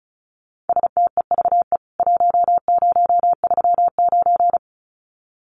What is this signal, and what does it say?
Morse code, heard as a keyed tone, about 34 wpm, 700 Hz. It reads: STEVE 1039
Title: CWT Exchange 35wpm